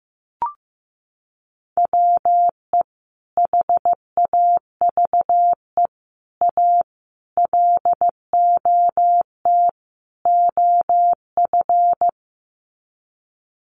Morse code, heard as a keyed tone, about 15 wpm, 700 Hz. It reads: WE HAVE A LOT OF
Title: Sets of 5 Words 15wpm